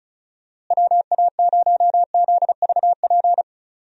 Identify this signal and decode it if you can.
Morse code, heard as a keyed tone, about 34 wpm, 700 Hz. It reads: WA0ZVP
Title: USA Call Signs - 6 Characters - Repeated 2x - 35wpm